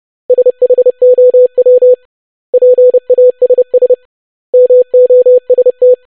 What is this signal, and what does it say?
Morse code, heard as a keyed tone, about 30 wpm, 500 Hz. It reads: SHOW PASS MOST
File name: basic500-25wpm